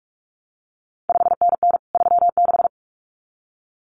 Morse code, heard as a keyed tone, about 46 wpm, 700 Hz. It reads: 5NN 36
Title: CQ WW DX Contest Exchange Encore - 45wpm